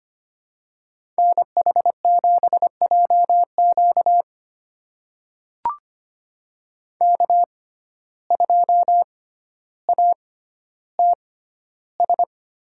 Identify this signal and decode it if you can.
Morse code, heard as a keyed tone, about 25 wpm, spaced as for 13 wpm, 700 Hz. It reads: NH7JQ K2ATS
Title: USA Call Signs - 5 Characters - 6x Character Spacing - 25wpm